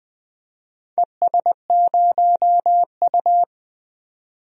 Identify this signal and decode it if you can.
Morse code, heard as a keyed tone, about 21 wpm, 700 Hz. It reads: ES0U
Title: DXCC-100 Call Signs - 4 Characters - 20wpm